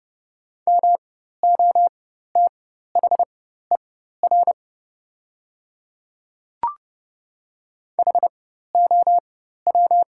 Morse code, heard as a keyed tone, about 30 wpm, spaced as for 14 wpm, 700 Hz. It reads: MOTHER HOW
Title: 4x Character Spacing - Top 300 Words - 30wpm